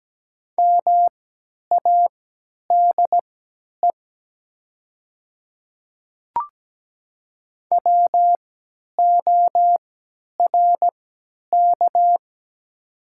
Morse code, heard as a keyed tone, about 17 wpm, spaced as for 10 wpm, 700 Hz. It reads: MADE WORK
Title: 3x Character Spacing - Top 200 Words - 17wpm